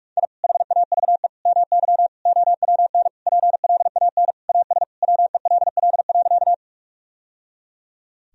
Morse code, heard as a keyed tone, about 46 wpm, 700 Hz. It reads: I HAVE MY OWN PLAN AS WELL.
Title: Sentences from Top 1000 Words - Review Part 2 - 45wpm